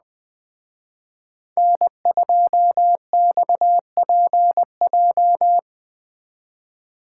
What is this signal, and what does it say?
Morse code, heard as a keyed tone, about 20 wpm, 700 Hz. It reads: N2XPJ
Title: Speed Racing - USA 1x3 Call Signs 17wpm